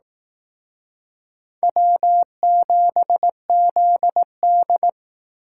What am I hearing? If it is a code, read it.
Morse code, heard as a keyed tone, about 18 wpm, 700 Hz. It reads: W7ZD
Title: Speed Racing - USA Call Signs 15wpm